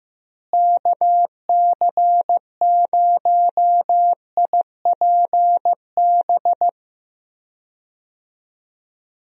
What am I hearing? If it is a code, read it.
Morse code, heard as a keyed tone, about 15 wpm, 700 Hz. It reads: KC0IPB
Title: USA Call Signs - 6 Characters - Repeated 2x - 15wpm